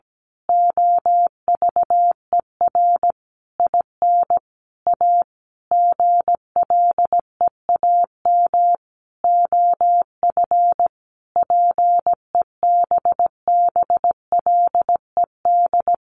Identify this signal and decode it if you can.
Morse code, heard as a keyed tone, about 17 wpm, 700 Hz. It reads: OVER IN A GLEAM OF PEBBLED
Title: A Sound Of Thunder 17wpm